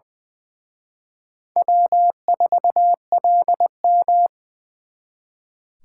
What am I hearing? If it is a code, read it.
Morse code, heard as a keyed tone, about 23 wpm, 700 Hz. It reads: W4LM
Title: Speed Racing - USA 1x2 Call Signs 20wpm